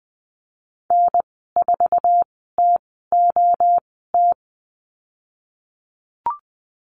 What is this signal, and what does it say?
Morse code, heard as a keyed tone, about 20 wpm, spaced as for 15 wpm, 700 Hz. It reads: N4TOT
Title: 088 Lesson 15g - Number 5 - Call Signs 20wpm